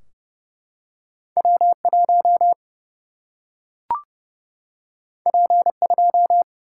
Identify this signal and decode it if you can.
Morse code, heard as a keyed tone, about 30 wpm, 700 Hz. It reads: W 1 P 2
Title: Single-Letter Single-Number 30wpm